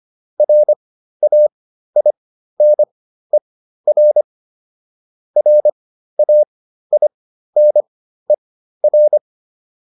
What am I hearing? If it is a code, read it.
Morse code, heard as a keyed tone, about 25 wpm, spaced as for 13 wpm, 600 Hz. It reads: RAINER RAINER
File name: QSO405 13